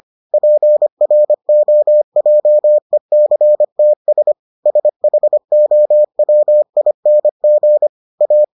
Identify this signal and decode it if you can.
Morse code, heard as a keyed tone, about 25 wpm, 600 Hz. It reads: PROJECTS SHOWING A